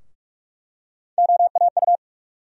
Morse code, heard as a keyed tone, about 46 wpm, 700 Hz. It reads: OAU
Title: USA Call Sign Postfixes - 3 Letters 45wpm